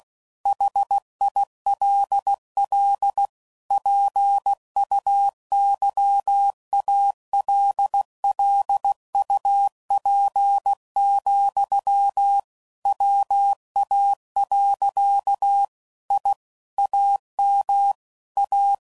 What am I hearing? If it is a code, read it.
Morse code, heard as a keyed tone, about 16 wpm, 800 Hz. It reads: HILL PUYALLUP, WA. I AM A
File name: CW Code Groups @15WPM / QSOs / kc7haa